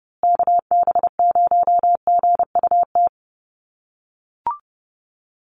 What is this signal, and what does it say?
Morse code, heard as a keyed tone, about 30 wpm, 700 Hz. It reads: KB0GUT
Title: USA Call Signs Repeated 2x 30wpm